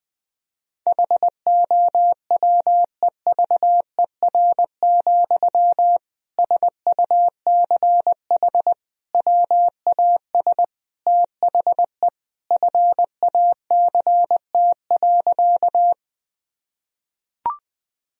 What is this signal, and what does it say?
Morse code, heard as a keyed tone, about 20 wpm, 700 Hz. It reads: HOWEVER, SUCH WAS THE FACT.
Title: Sentences from Top 1000 Words - Review Part 3 - 20wpm